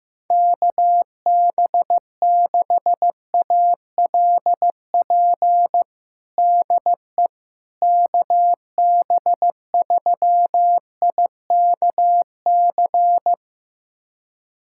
Morse code, heard as a keyed tone, about 15 wpm, 700 Hz. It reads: KB6ALP DE KB3IKC
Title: USA Callsign DE Callsign 15wpm